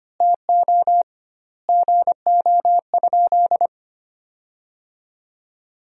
Morse code, heard as a keyed tone, about 25 wpm, 700 Hz. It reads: TO GO?
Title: 2x Word-Spacing - Sentences from Top 300 Words - 25wpm